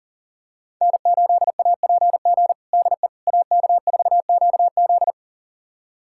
Morse code, heard as a keyed tone, about 40 wpm, 700 Hz. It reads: N8APG DE AK4QZ
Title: USA Callsign DE Callsign - 5 Characters - 40wpm